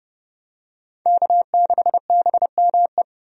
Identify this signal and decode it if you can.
Morse code, heard as a keyed tone, about 30 wpm, 700 Hz. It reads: K6BME
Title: USA Call Signs - 5 Characters - 30wpm